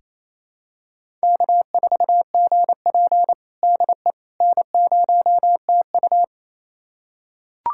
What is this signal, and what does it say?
Morse code, heard as a keyed tone, about 27 wpm, 700 Hz. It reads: K4GP DE N0TU
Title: USA Callsign DE Callsign - 4 Characters - 28wpm